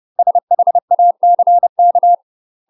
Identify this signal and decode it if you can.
Morse code, heard as a keyed tone, about 30 wpm, 700 Hz. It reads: SHACK
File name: UK MASTER QSOs TRAINING / 30WPM / UK MASTER QSOs 0042